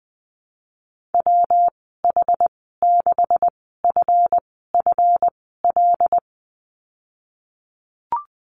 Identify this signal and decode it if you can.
Morse code, heard as a keyed tone, about 20 wpm, spaced as for 15 wpm, 700 Hz. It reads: WH6FFL
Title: 173 Lesson 28g - Number 9 - Call Signs 20wpm